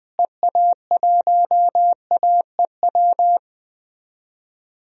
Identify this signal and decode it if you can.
Morse code, heard as a keyed tone, about 21 wpm, 700 Hz. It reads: EA1AEW
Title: DXCC-100 Call Signs - 6 Characters - Repeated 3x - 20wpm